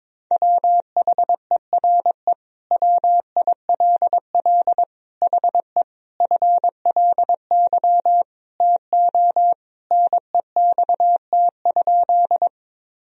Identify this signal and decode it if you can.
Morse code, heard as a keyed tone, about 22 wpm, 700 Hz. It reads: WHERE WILL HE FLY TO NEXT?
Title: Sentences from Top 1000 Words - Review Part 2 - 22wpm